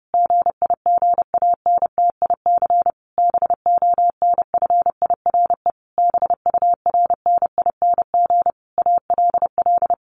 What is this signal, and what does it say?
Morse code, heard as a keyed tone, about 30 wpm, 700 Hz. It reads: GIGANTIC BONFIRE BURNING ALL
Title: A Sound Of Thunder 30wpm